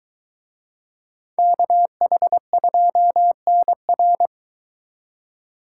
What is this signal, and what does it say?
Morse code, heard as a keyed tone, about 23 wpm, 700 Hz. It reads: KH2NR